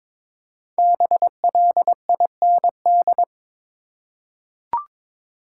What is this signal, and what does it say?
Morse code, heard as a keyed tone, about 22 wpm, 700 Hz. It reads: BLIND
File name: 5 Letter Words - First 3 Letters Given - 22wpm